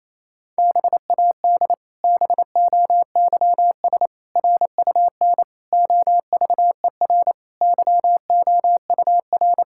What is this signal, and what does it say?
Morse code, heard as a keyed tone, about 28 wpm, 700 Hz. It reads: BAD BOYS RUN OVER YOUR
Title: Pangram Challenge v2 - 28wpm